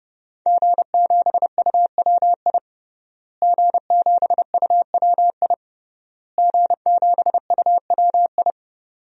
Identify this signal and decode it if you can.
Morse code, heard as a keyed tone, about 30 wpm, 700 Hz. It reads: G7UWI G7UWI G7UWI
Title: DXCC-100 Call Signs - 5 Characters - Repeated 3x - 30wpm